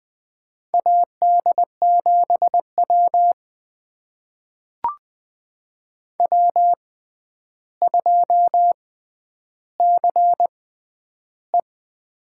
Morse code, heard as a keyed tone, about 20 wpm, spaced as for 10 wpm, 700 Hz. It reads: AD7W W2CE
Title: USA Call Signs - 4 Characters - 6x Character Spacing - 20wpm